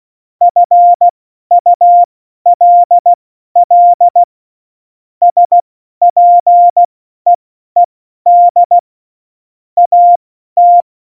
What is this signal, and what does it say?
Morse code, heard as a keyed tone, about 16 wpm, spaced as for 12 wpm, 700 Hz. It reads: FULL SPEED AT
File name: Morsekurs / Witze